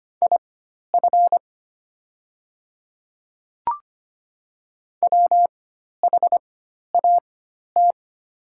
Morse code, heard as a keyed tone, about 25 wpm, spaced as for 12 wpm, 700 Hz. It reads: IF WHAT
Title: 4x Character Spacing - Top 100 Words - 25wpm